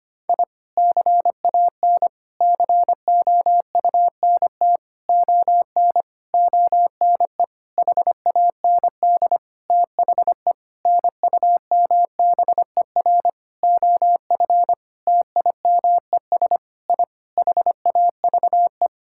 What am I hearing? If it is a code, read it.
Morse code, heard as a keyed tone, about 25 wpm, 700 Hz. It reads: I CAN COUNT ON ONE HAND THE NUMBER OF TIMES I HAVE
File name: practice / 25wpm / Sentences with 9-Word Phrases - 25wpm